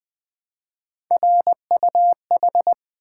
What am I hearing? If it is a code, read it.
Morse code, heard as a keyed tone, about 20 wpm, 700 Hz. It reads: RUH
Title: Three Letter-Number Combinations - 20wpm